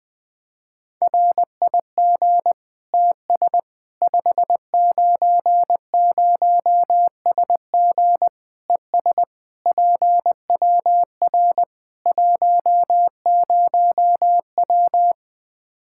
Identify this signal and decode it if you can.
Morse code, heard as a keyed tone, about 20 wpm, 700 Hz. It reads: RIG TS 590SG ES PWR 10W
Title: Practice QSO - Exchange 2 - Info - 20wpm